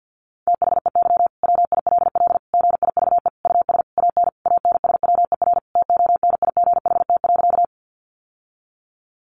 Morse code, heard as a keyed tone, about 49 wpm, 700 Hz. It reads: THEY WILL GIVE US AN ANSWER TONIGHT.
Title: Sentences from Top 600 Words 50wpm